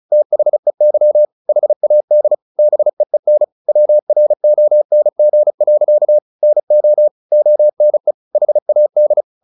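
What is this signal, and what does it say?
Morse code, heard as a keyed tone, about 34 wpm, 600 Hz. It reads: THEY HAD BEEN WRONG. NO ONE HAD
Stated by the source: Morse practice at 35 wpm from CW med Gnister, sending Ken Follett's novel Never in English.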